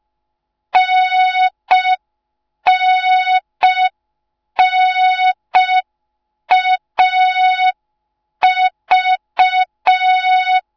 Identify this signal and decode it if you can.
Morse code, heard as a keyed tone, about 5 wpm, 3700 Hz. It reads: NNNAV